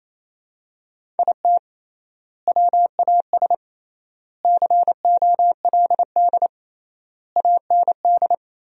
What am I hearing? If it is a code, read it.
Morse code, heard as a keyed tone, about 28 wpm, 700 Hz. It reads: IT WAS COLD AND
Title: 3x Word-Spacing - Sentences from Top 500 Words - 28wpm